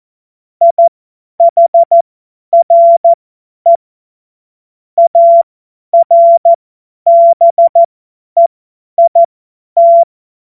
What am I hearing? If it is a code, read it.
Morse code, heard as a keyed tone, about 14 wpm, spaced as for 10 wpm, 680 Hz. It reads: IHRE ARBEIT
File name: Witz-12-August-2025-50-F70